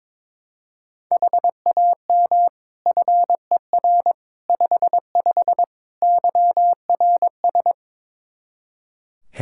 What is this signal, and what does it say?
Morse code, heard as a keyed tone, about 22 wpm, 700 Hz. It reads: HAM FER 55 YRS
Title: Practice QSO - Exchange 3 - More Info - 22wpm